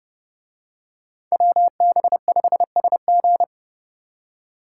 Morse code, heard as a keyed tone, about 28 wpm, 700 Hz. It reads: WB5SG